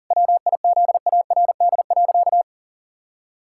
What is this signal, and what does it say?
Morse code, heard as a keyed tone, about 40 wpm, 700 Hz. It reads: WIZARD.
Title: Pangram Challenge v2 - 40wpm